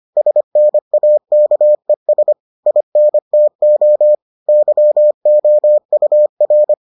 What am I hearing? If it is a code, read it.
Morse code, heard as a keyed tone, about 25 wpm, 600 Hz. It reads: SNAKES INTO YOUR